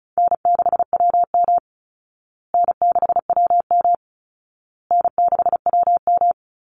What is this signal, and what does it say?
Morse code, heard as a keyed tone, about 34 wpm, 700 Hz. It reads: N6WM N6WM N6WM
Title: Easy Call Signs Repeated 3x - 35wpm